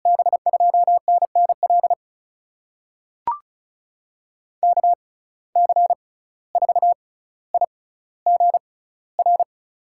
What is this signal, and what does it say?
Morse code, heard as a keyed tone, about 34 wpm, spaced as for 17 wpm, 700 Hz. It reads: B2NNL KC4IGR
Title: USA Call Signs - 6 Characters - 6x Character Spacing - 35wpm